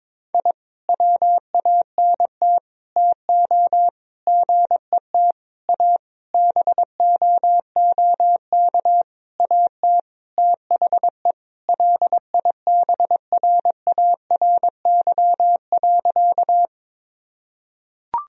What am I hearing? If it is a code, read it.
Morse code, heard as a keyed tone, about 22 wpm, 700 Hz. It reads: I WANT TO GET A BOOK AT THE LIBRARY.
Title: Sentences from Top 1000 Words - Review Part 1 - 22wpm